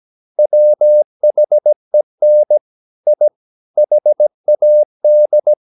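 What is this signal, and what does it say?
Morse code, heard as a keyed tone, about 17 wpm, 600 Hz. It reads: WHEN I HAD
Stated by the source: 17 wpm Morse practice sending the English text of a novel.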